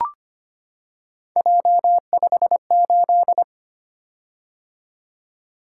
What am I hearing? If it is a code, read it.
Morse code, heard as a keyed tone, about 25 wpm, 700 Hz. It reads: J58
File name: Three Letter-Number Combinations - 25wpm